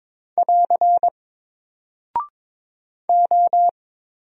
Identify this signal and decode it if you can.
Morse code, heard as a keyed tone, about 22 wpm, 700 Hz. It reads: <AR> O
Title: Single Character (Letter:Number:Punctuation:Prosign) Alt 22wpm